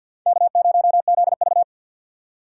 Morse code, heard as a keyed tone, about 46 wpm, 700 Hz. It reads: K0ZV
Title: USA Call Signs - 4 Characters - 3x Character Spacing - 50wpm